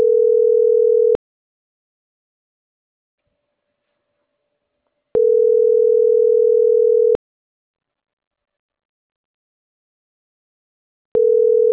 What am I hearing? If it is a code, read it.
Ringback tone (North America), 440 + 480 Hz.